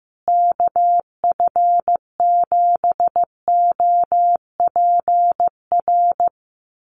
Morse code, heard as a keyed tone, about 15 wpm, 700 Hz. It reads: KF7OPR
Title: USA Call Signs Repeated 2x 15wpm